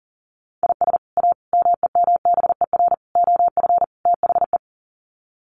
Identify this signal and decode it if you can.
Morse code, heard as a keyed tone, about 40 wpm, 700 Hz. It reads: IS A MEMBER OF THE
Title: Sets of 5 Words 40wpm